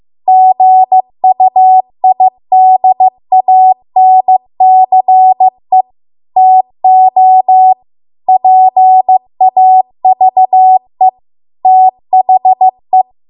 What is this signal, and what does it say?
Morse code, heard as a keyed tone, about 15 wpm, 750 Hz. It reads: GUIDANCE TO PAVE THE